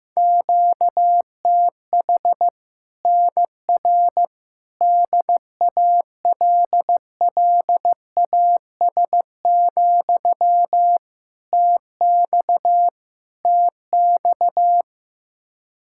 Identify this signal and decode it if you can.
Morse code, heard as a keyed tone, about 15 wpm, 700 Hz. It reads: QTH NR DALLAS, TX TX
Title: Practice QSO - Exchange 1 - Beginning - 15wpm